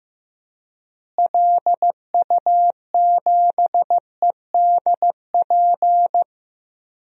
Morse code, heard as a keyed tone, about 15 wpm, 700 Hz. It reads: LU7EDP
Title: DXCC-100 Call Signs - 6 Characters - Repeated 2x - 15wpm